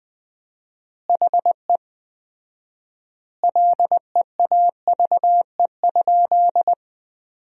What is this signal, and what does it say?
Morse code, heard as a keyed tone, about 20 wpm, 700 Hz. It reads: HE LEAVE?
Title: 4x Word-Spacing - Sentences from Top 400 Words - 20wpm